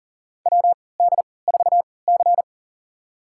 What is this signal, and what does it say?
Morse code, heard as a keyed tone, about 40 wpm, spaced as for 23 wpm, 700 Hz. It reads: WD4C
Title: USA Call Signs - 4 Characters - 3x Character Spacing - 40wpm